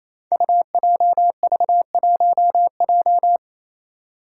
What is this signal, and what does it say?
Morse code, heard as a keyed tone, about 28 wpm, 700 Hz. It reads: UJV1J
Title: Five Challenging Letter-Number Combinations 28wpm